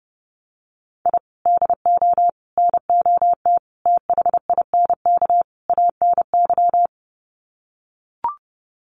Morse code, heard as a keyed tone, about 30 wpm, 700 Hz. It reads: I DO NOT THINK ANY
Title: Sets of 5 Words Encore 30wpm